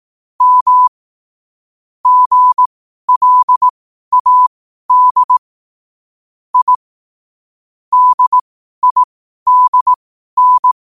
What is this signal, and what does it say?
Morse code, heard as a keyed tone, about 18 wpm, spaced as for 12 wpm, 1000 Hz. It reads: M GLAD I DIDN
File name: qotdfiles / 12wpm